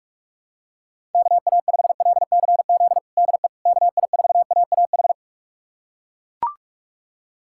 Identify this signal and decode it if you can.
Morse code, heard as a keyed tone, about 41 wpm, 700 Hz. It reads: KA5LCZ DE KI4AAH
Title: USA Callsign DE Callsign - 6 Characters - 45wpm